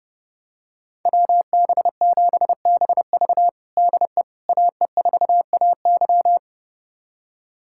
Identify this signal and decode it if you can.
Morse code, heard as a keyed tone, about 32 wpm, 700 Hz. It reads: WB7BV DE AE4AY
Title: USA Callsign DE Callsign - 5 Characters - 30wpm